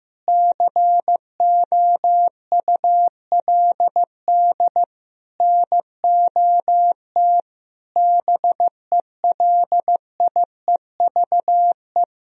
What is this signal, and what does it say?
Morse code, heard as a keyed tone, about 15 wpm, 700 Hz. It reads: COULD NOT BELIEVE
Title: Sentences from Top 1000 Words - Review Part 3 - 15wpm